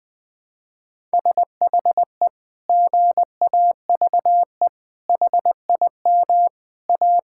Morse code, heard as a keyed tone, about 20 wpm, 700 Hz. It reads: SHE GAVE HIM A
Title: Sentences from Top 1600 Words 20wpm